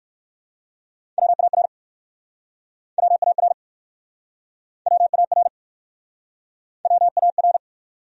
Morse code, heard as a keyed tone, about 60 wpm, 700 Hz. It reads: WAR WAR WAR WAR